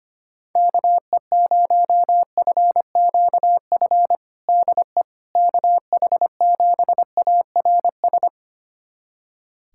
Morse code, heard as a keyed tone, about 25 wpm, 700 Hz. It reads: KE0FQF DE KH7ARS
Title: USA Callsign DE Callsign - 6 Characters - 25wpm